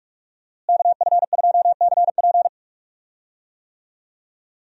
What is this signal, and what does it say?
Morse code, heard as a keyed tone, about 43 wpm, 700 Hz. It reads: KF2CP
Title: USA Call Signs - 5 Characters - Repeated 3x - 45wpm